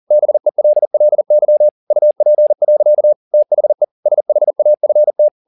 Morse code, heard as a keyed tone, about 40 wpm, 600 Hz. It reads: BELLY UP. THE SHAFT
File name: the old man and the sea 2 40wpm